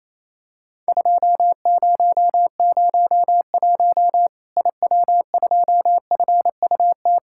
Morse code, heard as a keyed tone, about 28 wpm, 700 Hz. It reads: 2001 IW2FUT
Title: Worked All Europe DX Contest - QTC Exchange - 28wpm